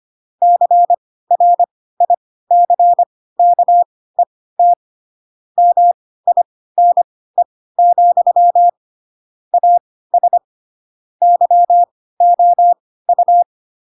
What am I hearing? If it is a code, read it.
Morse code, heard as a keyed tone, about 25 wpm, spaced as for 16 wpm, 700 Hz. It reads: CRICKET MINE, AS YOU
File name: Pinocchio / Pinocchio ch4 10 25 ik1hge